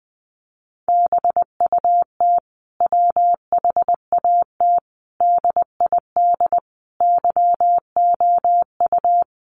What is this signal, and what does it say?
Morse code, heard as a keyed tone, about 20 wpm, 700 Hz. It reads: BUT WHAT DID YOU
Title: Sentences from Top 1300 Words 20wpm